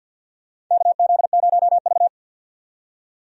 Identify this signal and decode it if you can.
Morse code, heard as a keyed tone, about 47 wpm, 700 Hz. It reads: KZ0V
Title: USA Call Signs - 4 Characters - 50wpm